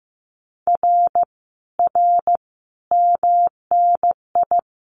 Morse code, heard as a keyed tone, about 15 wpm, 700 Hz. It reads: R R MNI
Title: Common QSO Phrases Encore 15wpm